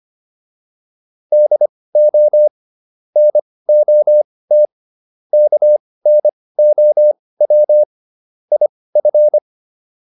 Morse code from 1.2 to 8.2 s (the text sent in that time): DO NOT KNOW